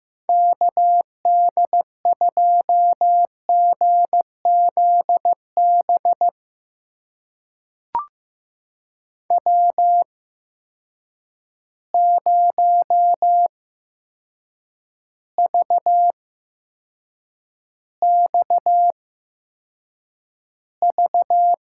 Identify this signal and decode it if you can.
Morse code, heard as a keyed tone, about 15 wpm, spaced as for 7 wpm, 700 Hz. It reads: KD2GZB W0VXV
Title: USA Call Signs - 8x Character Spacing - 15wpm